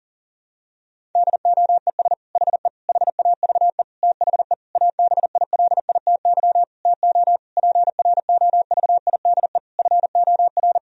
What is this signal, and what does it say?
Morse code, heard as a keyed tone, about 40 wpm, 700 Hz. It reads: DOES HE HAVE THE ABILITY TO PROVIDE FOR
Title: Sentences from Top 1000 Words - Review Part 3 - 40wpm